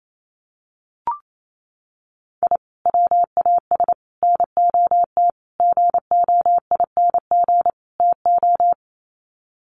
Morse code, heard as a keyed tone, about 28 wpm, 700 Hz. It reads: I WAS NOT GOING TO
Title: Sets of 5 Words 28wpm